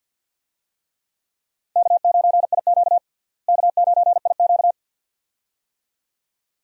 Morse code, heard as a keyed tone, about 48 wpm, 700 Hz. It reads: K9IQ K9IQ
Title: USA Call Signs - 4 Characters - Repeated 2x - 50wpm